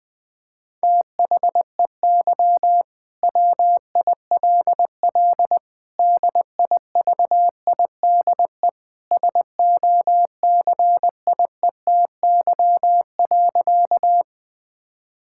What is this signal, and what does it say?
Morse code, heard as a keyed tone, about 20 wpm, 700 Hz. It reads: THEY WILL DIVIDE SOCIETY.
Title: Sentences from Top 1600 Words 20wpm